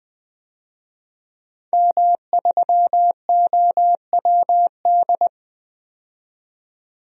Morse code, heard as a keyed tone, about 20 wpm, 700 Hz. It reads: M3OWD
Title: DXCC-100 Call Signs - 5 Characters - 20wpm